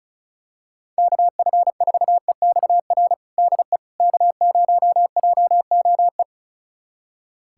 Morse code, heard as a keyed tone, about 36 wpm, 700 Hz. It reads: KF4EXR DE K0JOE
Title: USA Callsign DE Callsign 35wpm